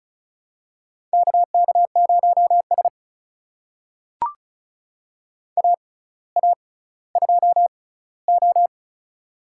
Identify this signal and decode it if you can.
Morse code, heard as a keyed tone, about 35 wpm, spaced as for 18 wpm, 700 Hz. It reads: KK0S AA2O
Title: USA Call Signs - 4 Characters - 6x Character Spacing - 35wpm